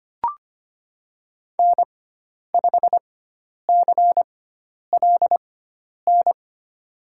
Morse code, heard as a keyed tone, about 25 wpm, spaced as for 10 wpm, 700 Hz. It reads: N5CLN
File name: Lesson 16g - Letter c - Call Signs 25wpm